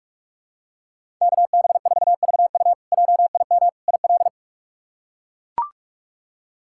Morse code, heard as a keyed tone, about 42 wpm, 700 Hz. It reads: KB4VU JIM IL
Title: K1USN SST QSO Practice - 45wpm